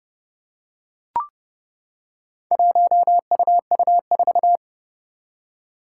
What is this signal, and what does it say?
Morse code, heard as a keyed tone, about 30 wpm, 700 Hz. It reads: 1UU4
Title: Four Challenging Letter-Number Combinations 30wpm